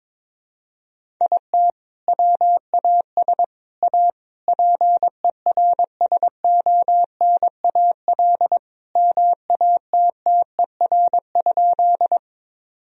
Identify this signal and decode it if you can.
Morse code, heard as a keyed tone, about 22 wpm, 700 Hz. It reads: IT WAS A PERSONAL MATTER?
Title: Sentences from Top 1000 Words - Review Part 4 - 22wpm